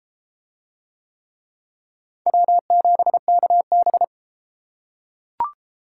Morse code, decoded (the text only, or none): W7KB